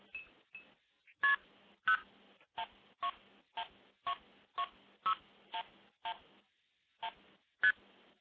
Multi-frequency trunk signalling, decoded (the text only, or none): KP011 6111ST